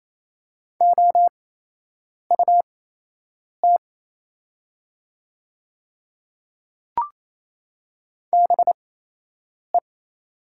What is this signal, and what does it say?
Morse code, heard as a keyed tone, about 28 wpm, spaced as for 8 wpm, 700 Hz. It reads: OUT BE
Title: 8x Character Spacing - Top 100 Words - 28wpm